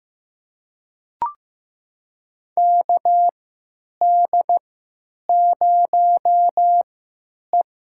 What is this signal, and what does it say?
Morse code, heard as a keyed tone, about 15 wpm, spaced as for 8 wpm, 700 Hz. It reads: KD0E
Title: USA Call Signs - 3x Character Spacing - 15wpm